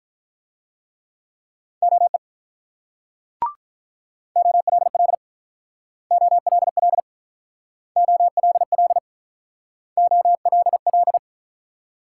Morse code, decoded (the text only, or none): OE OLL OLL OLL OLL